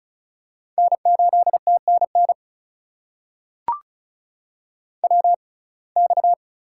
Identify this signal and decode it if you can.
Morse code, heard as a keyed tone, about 36 wpm, spaced as for 25 wpm, 700 Hz. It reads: N8TNN WX
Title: USA Call Signs - 6x Character Spacing - 35wpm